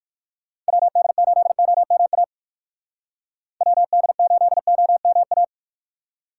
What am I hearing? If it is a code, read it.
Morse code, heard as a keyed tone, about 58 wpm, 700 Hz. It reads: WD8OMA WD8OMA